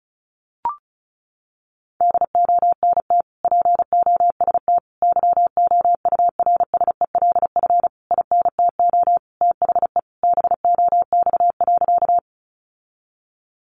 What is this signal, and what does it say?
Morse code, heard as a keyed tone, about 34 wpm, 700 Hz. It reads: DONT POST YOURSELF INTO THE BOX.
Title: Sentences from Top 1000 Words 35wpm